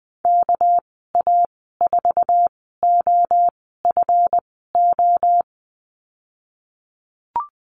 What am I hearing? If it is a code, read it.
Morse code, heard as a keyed tone, about 20 wpm, spaced as for 14 wpm, 700 Hz. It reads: KA4OFO
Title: 245 Lesson 39g - Letter z - Call Signs 20wpm